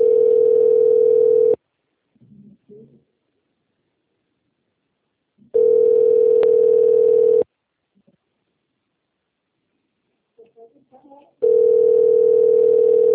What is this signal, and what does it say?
Ringback tone (North America), 440 + 480 Hz, with speech.
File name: rg-1600-unknown- - - .2920